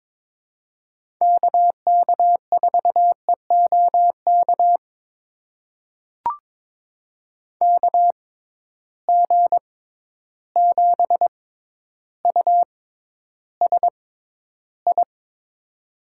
Morse code, heard as a keyed tone, about 21 wpm, spaced as for 11 wpm, 700 Hz. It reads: KK4EOK KG7USI
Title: USA Call Signs - 6x Character Spacing - 22wpm